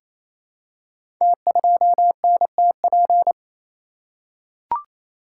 Morse code, heard as a keyed tone, about 29 wpm, 700 Hz. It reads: T2NTP
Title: Code Groups - 5 Letter:Numbers - 28wpm